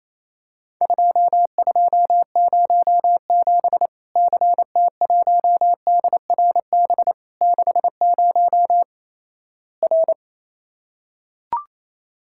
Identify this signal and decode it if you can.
Morse code, heard as a keyed tone, about 27 wpm, 700 Hz. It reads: 2207 CT1DRB 60 R
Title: Worked All Europe DX Contest - QTC Exchange - 28wpm